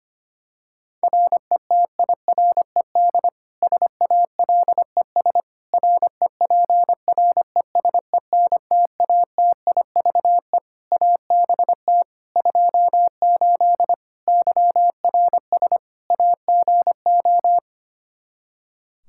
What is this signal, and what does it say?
Morse code, heard as a keyed tone, about 25 wpm, 700 Hz. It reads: RETIRED SALES REPRESENTATIVE ABT 28 YRS AGO
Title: Practice QSO - Exchange 3 - More Info - 25wpm